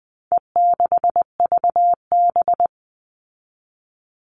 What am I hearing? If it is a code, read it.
Morse code, heard as a keyed tone, about 20 wpm, 700 Hz. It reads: E6VB
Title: All Cut Numbers - 4-Digit 20wpm